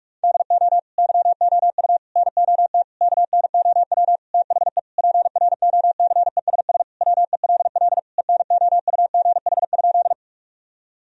Morse code, heard as a keyed tone, about 46 wpm, 700 Hz. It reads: DO YOU NOT KNOW THE PROCESS WELL ENOUGH?
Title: Sentences from Top 1000 Words - Review Part 3 - 45wpm